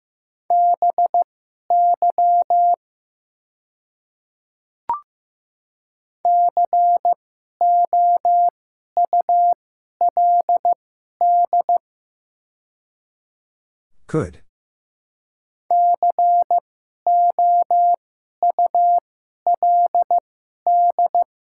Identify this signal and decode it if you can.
Morse code, heard as a keyed tone, about 15 wpm, spaced as for 11 wpm, 700 Hz. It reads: BY COULD COULD
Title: 2x Character Spacing - Top 200 Words - 15wpm